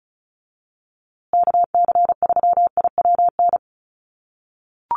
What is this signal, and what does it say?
Morse code, heard as a keyed tone, about 34 wpm, 700 Hz. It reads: KC3IWN
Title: USA Call Signs Repeated 2x 35wpm